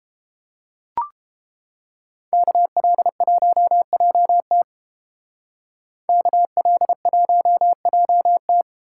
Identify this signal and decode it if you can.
Morse code, heard as a keyed tone, about 30 wpm, 700 Hz. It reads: KL1JT KL1JT